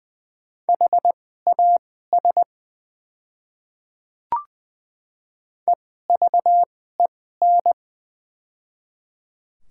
Morse code, heard as a keyed tone, about 21 wpm, spaced as for 15 wpm, 700 Hz. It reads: HAS EVEN
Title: 2x Character Spacing - Top 200 Words - 20wpm